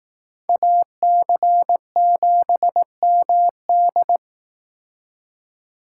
Morse code, heard as a keyed tone, about 18 wpm, 700 Hz. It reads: AC7MD
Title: Speed Racing - USA 2x2 Call Signs 15wpm